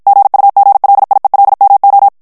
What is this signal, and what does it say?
Morse code, heard as a keyed tone, about 52 wpm, 800 Hz. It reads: GUGLIELMO
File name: morse mp3 / ITALIEN texte clair 1 vitesse 50 mots minute